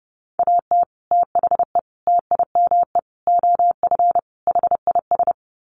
Morse code, heard as a keyed tone, about 30 wpm, 700 Hz. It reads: AT THE TIME OF HIS
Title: Sets of 5 Words Encore2 30wpm